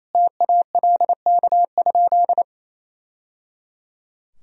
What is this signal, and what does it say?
Morse code, heard as a keyed tone, about 28 wpm, 700 Hz. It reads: TALK?
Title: Sentences from Top 1000 Words - Review Part 1 - 28wpm